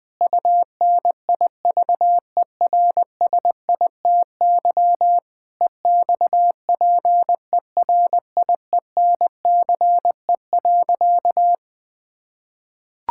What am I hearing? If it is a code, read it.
Morse code, heard as a keyed tone, about 20 wpm, 700 Hz. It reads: UNIVERSITY EXPERIENCE.
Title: Sentences from Top 1000 Words - Review Part 5 - 20wpm